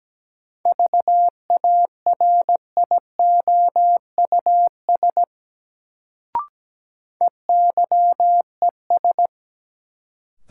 Morse code, heard as a keyed tone, about 17 wpm, 700 Hz. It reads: VARIOUS EYES
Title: Sentences from Top 1000 Words - Review Part 4 - New Word Drill - 17wpm